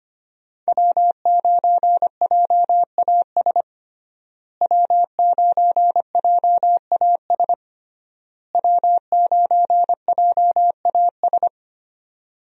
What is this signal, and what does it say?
Morse code, heard as a keyed tone, about 25 wpm, 700 Hz. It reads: W9JAS W9JAS W9JAS
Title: USA Call Signs - 5 Characters - Repeated 3x - 25wpm